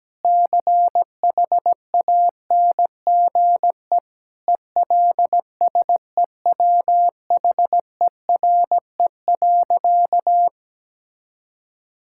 Morse code, heard as a keyed tone, about 17 wpm, 700 Hz. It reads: CHANGE ELSEWHERE.
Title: Sentences from Top 1600 Words 17wpm